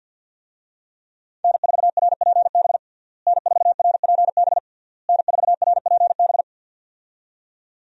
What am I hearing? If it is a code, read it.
Morse code, heard as a keyed tone, about 50 wpm, 700 Hz. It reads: N4RPB N4RPB N4RPB
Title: USA Call Signs - 5 Characters - Repeated 3x - 50wpm